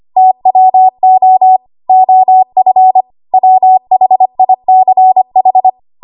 Morse code, heard as a keyed tone, about 25 wpm, 750 Hz. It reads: TWO OF WHICH
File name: Archive / 25 WPM / 25WPM